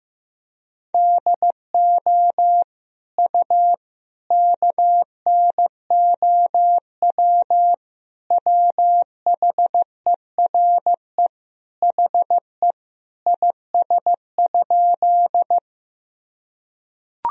Morse code, heard as a keyed tone, about 16 wpm, 700 Hz. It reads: DO U KNOW WHERE HE IS?
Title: Sentences from Top 300 Words and Abbreviations 15wpm